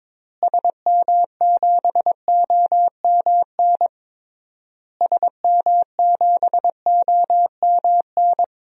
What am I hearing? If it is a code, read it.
Morse code, heard as a keyed tone, about 22 wpm, 700 Hz. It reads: SM7OMN SM7OMN
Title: DXCC-100 Call Signs - 6 Characters - Repeated 3x - 22wpm